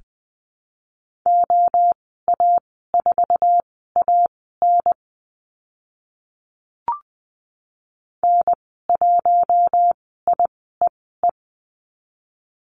Morse code, heard as a keyed tone, about 20 wpm, spaced as for 15 wpm, 700 Hz. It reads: OA4AN N1IEE
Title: 048 Lesson 9g - Number 4 - Call Signs 20wpm